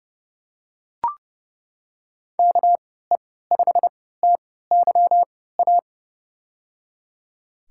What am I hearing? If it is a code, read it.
Morse code, heard as a keyed tone, about 30 wpm, spaced as for 17 wpm, 700 Hz. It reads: KE5TYA
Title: USA Call Signs - 6 Characters - 3x Character Spacing - 30wpm